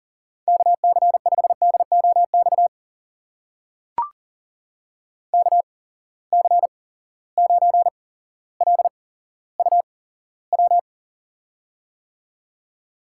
Morse code, heard as a keyed tone, about 40 wpm, spaced as for 17 wpm, 700 Hz. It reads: KC5DOX KC9LUW
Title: USA Call Signs - 6 Characters - 8x Character Spacing - 40wpm